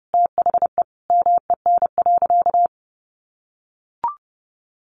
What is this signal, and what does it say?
Morse code, heard as a keyed tone, about 30 wpm, 700 Hz. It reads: THE MEN.
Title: Sentences from Top 1400 Words 30wpm